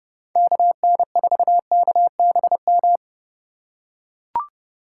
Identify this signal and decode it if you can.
Morse code, heard as a keyed tone, about 30 wpm, 700 Hz. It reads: KN4KBM
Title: USA Call Signs - 3x Character Spacing - 30wpm